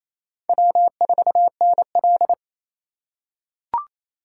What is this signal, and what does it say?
Morse code, heard as a keyed tone, about 28 wpm, 700 Hz. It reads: W4NL
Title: USA Call Signs - 2x Character Spacing - 28wpm